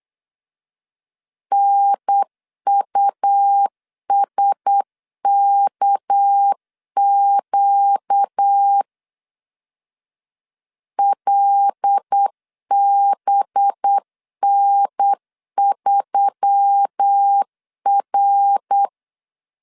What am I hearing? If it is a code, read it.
Morse code, heard as a keyed tone, about 8 wpm, 800 Hz. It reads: NUSKQ LBN3R
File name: LEZIONI / 6 PKX0 / 6 PKX 8wpm 3041